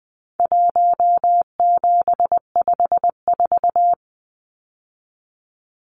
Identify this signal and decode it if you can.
Morse code, heard as a keyed tone, about 20 wpm, 700 Hz. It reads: 1754
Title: Numbers - Contest Style - 4-Digit 20wpm